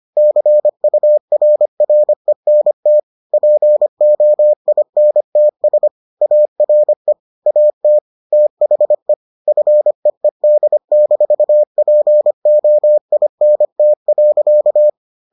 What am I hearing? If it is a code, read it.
Morse code, heard as a keyed tone, about 25 wpm, 600 Hz. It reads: CURRENT POINTS ARE AT THE FEED-POINT.